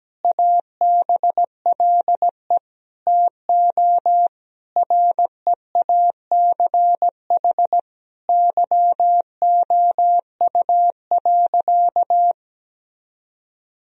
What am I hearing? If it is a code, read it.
Morse code, heard as a keyed tone, about 17 wpm, 700 Hz. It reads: ABLE TO REACH YOU.
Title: Sentences from Top 1000 Words - Review Part 2 - 17wpm